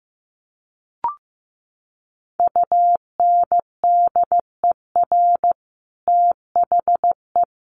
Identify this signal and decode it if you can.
Morse code, heard as a keyed tone, about 15 wpm, 700 Hz. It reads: UNDER THE
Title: Sets of 5 Words Encore2 15wpm